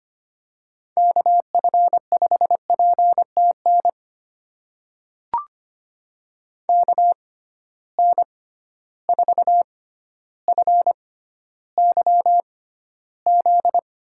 Morse code, heard as a keyed tone, about 25 wpm, spaced as for 13 wpm, 700 Hz. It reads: KF5PTN KN4FYZ
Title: USA Call Signs - 6 Characters - 6x Character Spacing - 25wpm